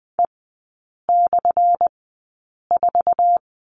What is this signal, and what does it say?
Morse code, heard as a keyed tone, about 20 wpm, spaced as for 8 wpm, 700 Hz. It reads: E/4
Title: 191 Lesson 31e - Letter v - Three Characters 20wpm